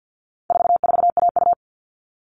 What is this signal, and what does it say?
Morse code, heard as a keyed tone, about 48 wpm, 700 Hz. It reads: 44AU
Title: All Cut Numbers - 4-Digit 50wpm